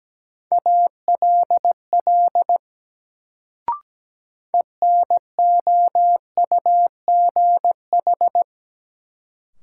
Morse code, heard as a keyed tone, about 17 wpm, 700 Hz. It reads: ALL ENOUGH
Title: Sets of 2 Words - New Word Drill - 17wpm